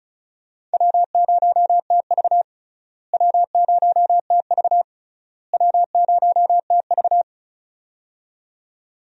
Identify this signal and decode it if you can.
Morse code, heard as a keyed tone, about 34 wpm, 700 Hz. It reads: W0TV W0TV W0TV
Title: USA Call Signs - 4 Characters - Repeated 3x - 35wpm